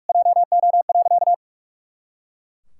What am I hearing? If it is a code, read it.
Morse code, heard as a keyed tone, about 41 wpm, 700 Hz. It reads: JO.
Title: Pangram Challenge v2 - 45wpm